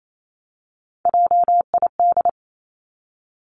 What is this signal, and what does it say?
Morse code, heard as a keyed tone, about 28 wpm, 700 Hz. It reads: JID
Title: Three Challenging Letter-Number Combinations 28wpm